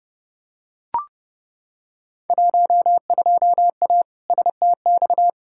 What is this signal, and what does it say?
Morse code, heard as a keyed tone, about 31 wpm, 700 Hz. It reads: 12A STX
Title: Field Day Exchanges Encore - 30wpm